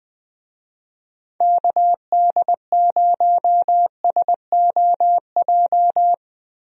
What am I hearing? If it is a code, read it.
Morse code, heard as a keyed tone, about 21 wpm, 700 Hz. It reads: KD0SOJ
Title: USA Call Signs - 6 Characters - Repeated 2x - 20wpm